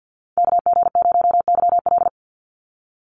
Morse code, heard as a keyed tone, about 50 wpm, 700 Hz. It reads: KG0YL
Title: Call Signs - 50wpm